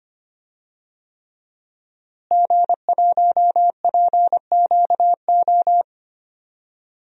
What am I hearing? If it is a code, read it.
Morse code, heard as a keyed tone, about 25 wpm, 700 Hz. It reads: G1PQO
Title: DXCC-100 Call Signs - 5 Characters - Repeated 3x - 25wpm